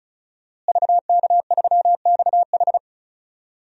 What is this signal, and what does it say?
Morse code, heard as a keyed tone, about 36 wpm, 700 Hz. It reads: VK3XH
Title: DXCC-100 Call Signs - 5 Characters - Repeated 3x - 35wpm